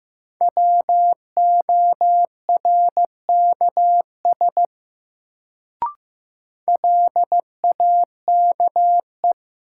Morse code, heard as a keyed tone, about 15 wpm, 700 Hz. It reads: WORKS LAKE
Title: Sentences from Top 1100 Words - New Word Drill Alt - 15wpm